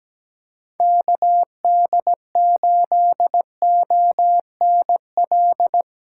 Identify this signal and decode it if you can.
Morse code, heard as a keyed tone, about 17 wpm, 700 Hz. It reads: KD8ONL
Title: USA Call Signs - 6 Characters - 3x Character Spacing - 17wpm